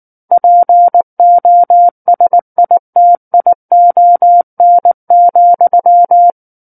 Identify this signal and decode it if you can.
Morse code, heard as a keyed tone, about 19 wpm, 700 Hz. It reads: POSITION,
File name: Book One / 10-Chapter 10 - 19 WPM